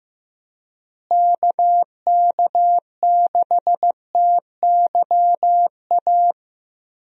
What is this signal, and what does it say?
Morse code, heard as a keyed tone, about 15 wpm, 700 Hz. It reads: KK6TYA
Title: USA Call Signs - 6 Characters - Repeated 2x - 15wpm